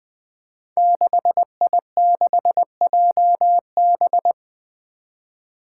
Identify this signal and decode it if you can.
Morse code, heard as a keyed tone, about 20 wpm, 700 Hz. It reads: 6I6JB
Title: Five Challenging Letter-Number Combinations 20wpm